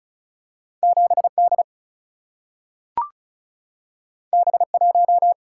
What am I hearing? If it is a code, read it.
Morse code, heard as a keyed tone, about 36 wpm, 700 Hz. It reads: 7 D B 1
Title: Single-Letter Single-Number 35wpm